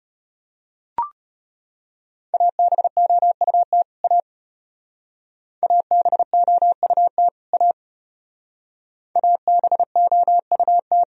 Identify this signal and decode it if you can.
Morse code, heard as a keyed tone, about 32 wpm, 700 Hz. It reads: ABOUT A ABOUT A ABOUT